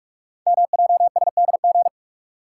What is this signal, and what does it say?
Morse code, heard as a keyed tone, about 46 wpm, 690 Hz. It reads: MJSDG
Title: Code Groups - 5 Letters - 45wpm